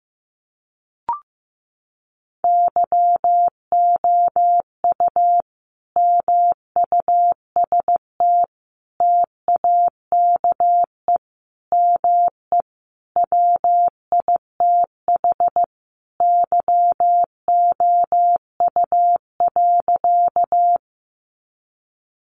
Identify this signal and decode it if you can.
Morse code, heard as a keyed tone, about 15 wpm, 700 Hz. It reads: YOU MUST TAKE ME WITH YOU.
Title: Sentences from Top 200 Words 15wpm